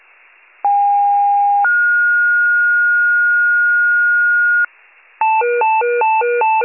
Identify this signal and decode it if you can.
Two-tone page, heard as a one-second tone, then a three-second tone, about 810 then 1430 Hz, with noise. It is heard from 0.6 to 4.7 s.